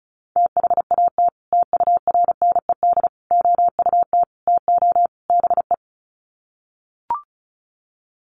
Morse code, heard as a keyed tone, about 34 wpm, 700 Hz. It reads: THAT TURNED OUT TO BE
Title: Sets of 5 Words Encore2 35wpm